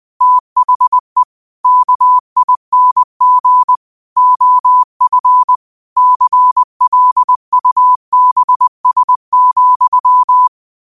Morse code, heard as a keyed tone, about 20 wpm, 1000 Hz. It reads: THE KING OF CLUBS,
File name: qotdfiles / 20wpm